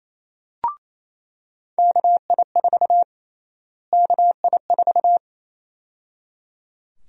Morse code, heard as a keyed tone, about 28 wpm, 700 Hz. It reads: KI4 KI4
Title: USA Call Sign Prefixes - Repeated 2x - 28wpm